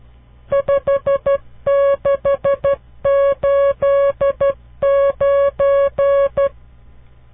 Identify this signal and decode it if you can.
Morse code, heard as a keyed tone, about 12 wpm, 560 Hz. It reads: 5689
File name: CW Learning Programs / farnsworth / unit 5